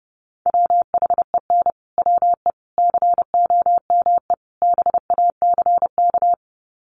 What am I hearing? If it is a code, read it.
Morse code, heard as a keyed tone, about 30 wpm, 700 Hz. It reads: WHEN WE COME BACK
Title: Sets of 4 Words 30wpm